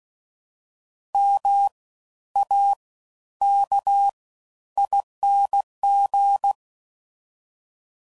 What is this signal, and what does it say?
Morse code, heard as a keyed tone, about 16 wpm, spaced as for 11 wpm, 800 Hz. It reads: MAKING
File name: Suffixes / Special INGs / Fast Ings